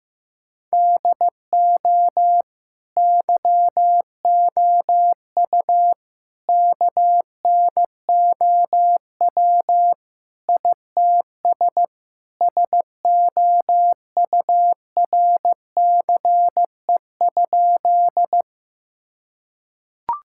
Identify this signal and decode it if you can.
Morse code, heard as a keyed tone, about 16 wpm, 700 Hz. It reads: DO YOU KNOW ITS SOURCE?
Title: Sentences from Top 1000 Words - Review Part 5 - 15wpm